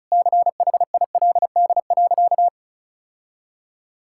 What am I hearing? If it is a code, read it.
Morse code, heard as a keyed tone, about 34 wpm, 700 Hz. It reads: CHILD.
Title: Sentences from Top 1000 Words - Review Part 5 - 35wpm